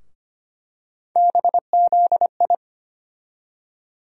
Morse code, heard as a keyed tone, about 25 wpm, 700 Hz. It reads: BZI
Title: USA Call Sign Postfixes - 3 Letters 25wpm